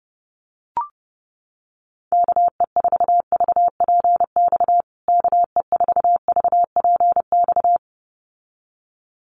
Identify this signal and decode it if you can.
Morse code, heard as a keyed tone, about 30 wpm, 700 Hz. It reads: KE4VPX KE4VPX
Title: USA Call Signs Repeated 2x 30wpm